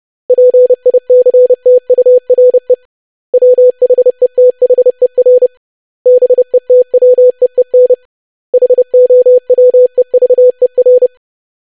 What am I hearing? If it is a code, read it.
Morse code, heard as a keyed tone, about 30 wpm, 500 Hz. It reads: PICTURE WHETHER BETWEEN HOWEVER
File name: lesson / web / basic500-25wpm